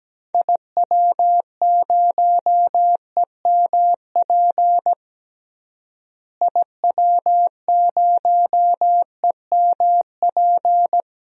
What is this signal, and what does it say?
Morse code, heard as a keyed tone, about 17 wpm, 700 Hz. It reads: IW0EMP IW0EMP
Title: DXCC-100 Call Signs - 6 Characters - Repeated 3x - 17wpm